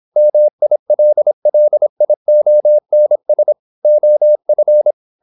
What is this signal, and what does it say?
Morse code, heard as a keyed tone, about 26 wpm, 600 Hz. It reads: MILLIONS OF
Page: Never 26wpm | CW med Gnister